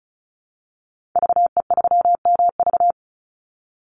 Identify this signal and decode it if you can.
Morse code, heard as a keyed tone, about 34 wpm, 700 Hz. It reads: VE3MV
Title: Call Signs Repeated 3x - 35wpm